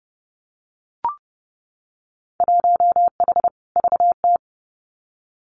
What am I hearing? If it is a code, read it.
Morse code, heard as a keyed tone, about 29 wpm, 700 Hz. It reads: 1H VT
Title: Winter Field Day Exchanges 30wpm